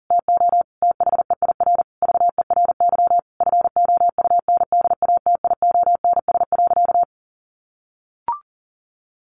Morse code, heard as a keyed tone, about 40 wpm, 700 Hz. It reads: TO THEIR VERY FOUNDATIONS.
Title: A Sound Of Thunder 40wpm